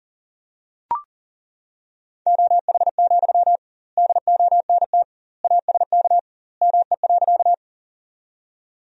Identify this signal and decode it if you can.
Morse code, heard as a keyed tone, about 40 wpm, 700 Hz. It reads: OH, DONT ASK ME.
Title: 2x Word-Spacing - Sentences from Top 500 Words - 40wpm